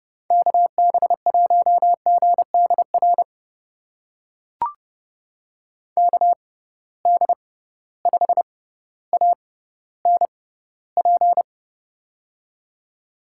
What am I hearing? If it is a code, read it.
Morse code, heard as a keyed tone, about 30 wpm, spaced as for 15 wpm, 700 Hz. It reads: KB1GDR KD5ANP
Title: USA Call Signs - 6 Characters - 6x Character Spacing - 30wpm